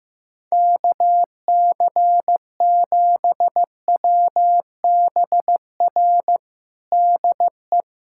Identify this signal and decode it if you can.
Morse code, heard as a keyed tone, about 15 wpm, 700 Hz. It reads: KC7WBR DE
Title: USA Callsign DE Callsign - 6 Characters - 15wpm